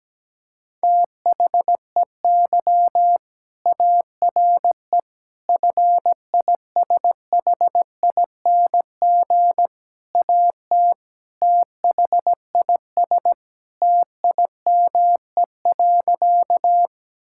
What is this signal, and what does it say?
Morse code, heard as a keyed tone, about 17 wpm, 700 Hz. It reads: THEY ARE FISHING AT THIS TIME.
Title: Sentences from Top 1000 Words - Review Part 5 - 17wpm